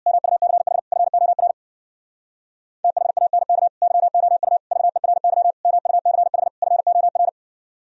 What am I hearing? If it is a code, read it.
Morse code, heard as a keyed tone, about 62 wpm, 700 Hz. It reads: MUCH FOR THANK YOU VERY MUCH FOR
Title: Speed-Racing - Sets of 5 Words 50wpm